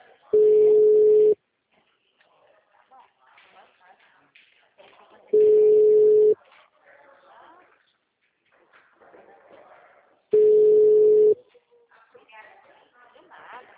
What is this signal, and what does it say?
Ringback tone (Europe, 425 Hz), with speech.